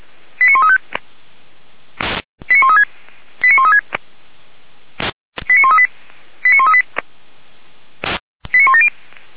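Five-tone selective call (ZVEI), heard as quick runs of five tones, tones of about 80 ms. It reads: 89135 89136 89136 89138 89138 89170
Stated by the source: ZVEI five-tone selective call received on an SDR.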